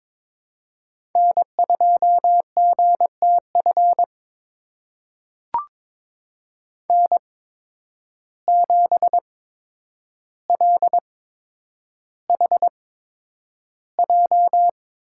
Morse code, heard as a keyed tone, about 22 wpm, spaced as for 9 wpm, 700 Hz. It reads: N2GTF N7LHJ
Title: USA Call Signs - 5 Characters - 8x Character Spacing - 22wpm